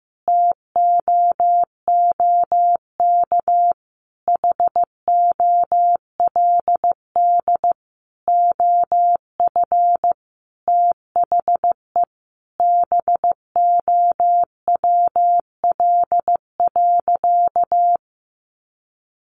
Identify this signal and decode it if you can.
Morse code, heard as a keyed tone, about 15 wpm, 700 Hz. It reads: TOOK HOLD OF THE BOWL.
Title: Sentences from Top 1200 Words 15wpm